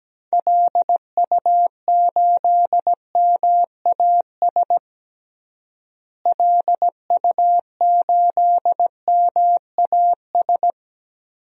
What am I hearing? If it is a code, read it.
Morse code, heard as a keyed tone, about 17 wpm, 700 Hz. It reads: LU8MAS LU8MAS
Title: DXCC-100 Call Signs - 6 Characters - Repeated 2x - 17wpm